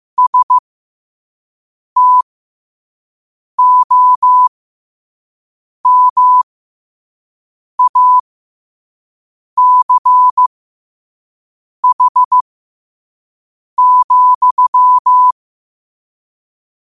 Morse code, heard as a keyed tone, about 15 wpm, spaced as for 5 wpm, 1000 Hz. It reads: STOMACH,
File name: qotdfiles / 5wpm